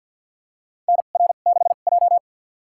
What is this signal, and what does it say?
Morse code, heard as a keyed tone, about 52 wpm, spaced as for 37 wpm, 700 Hz. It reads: NR6J
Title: USA Call Signs - 4 Characters - 2x Character Spacing - 50wpm